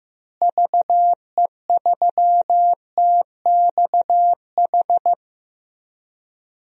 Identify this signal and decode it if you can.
Morse code, heard as a keyed tone, about 15 wpm, 700 Hz. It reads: VE3TXH
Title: DXCC-100 Call Signs - 6 Characters - Repeated 3x - 15wpm